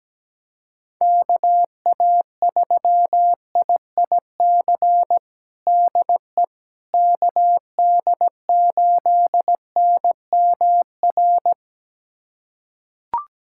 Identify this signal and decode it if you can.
Morse code, heard as a keyed tone, about 17 wpm, 700 Hz. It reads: KA3IIC DE KD8NMR
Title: USA Callsign DE Callsign - 6 Characters - 17wpm